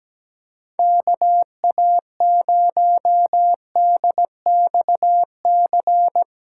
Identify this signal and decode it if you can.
Morse code, heard as a keyed tone, about 18 wpm, 700 Hz. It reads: KA0DXC
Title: Speed Racing - USA Call Signs 17wpm